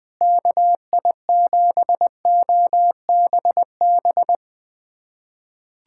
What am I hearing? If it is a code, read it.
Morse code, heard as a keyed tone, about 20 wpm, 700 Hz. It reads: KI7OBB
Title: USA Callsign DE Callsign - 6 Characters - 20wpm